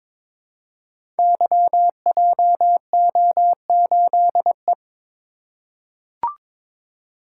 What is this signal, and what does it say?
Morse code, heard as a keyed tone, about 22 wpm, 700 Hz. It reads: YJO8E
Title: Code Groups - 5 Letter:Numbers - 22wpm